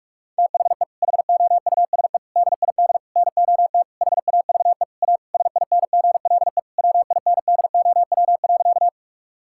Morse code, heard as a keyed tone, about 46 wpm, 700 Hz. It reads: THE HOUSE DID NOT HAVE A SINGLE WINDOW.
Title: Sentences from Top 1600 Words 45wpm